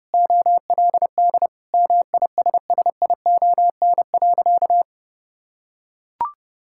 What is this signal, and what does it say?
Morse code, heard as a keyed tone, about 30 wpm, 700 Hz. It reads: OLD MISSION.
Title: Sentences from Top 1600 Words 30wpm